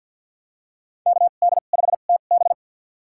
Morse code, heard as a keyed tone, about 47 wpm, spaced as for 35 wpm, 700 Hz. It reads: KD5TB
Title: USA Call Signs - 5 Characters - 2x Character Spacing - 50wpm